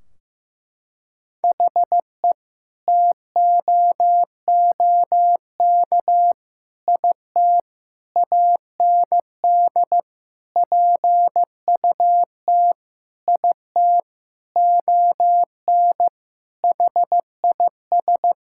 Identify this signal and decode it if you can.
Morse code, heard as a keyed tone, about 15 wpm, 700 Hz. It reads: HE TOOK IT AND PUT IT ON HIS
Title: Sentences from Top 1600 Words 15wpm